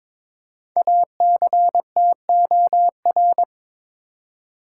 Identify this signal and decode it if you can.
Morse code, heard as a keyed tone, about 22 wpm, 700 Hz. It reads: ACTOR
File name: 5 Letter Words - First 3 Letters Given - 22wpm